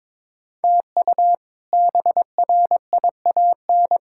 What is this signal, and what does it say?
Morse code, heard as a keyed tone, about 23 wpm, 700 Hz. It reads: TU BRIAN
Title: K1USN SST QSO Practice - 22wpm